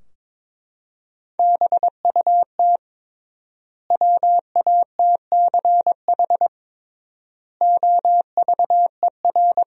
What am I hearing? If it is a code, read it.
Morse code, heard as a keyed tone, about 22 wpm, 700 Hz. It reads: BUT WATCH OVER
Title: 3x Word-Spacing - Sentences from Top 300 Words - 22wpm